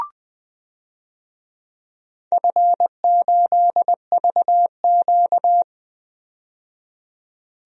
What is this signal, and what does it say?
Morse code, heard as a keyed tone, about 20 wpm, 700 Hz. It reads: F8VQ
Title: DXCC-100 Call Signs - 4 Characters - 20wpm